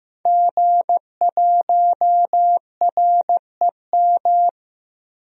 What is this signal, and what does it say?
Morse code, heard as a keyed tone, about 15 wpm, 700 Hz. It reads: G1REM
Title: DXCC-100 Call Signs - 5 Characters - 15wpm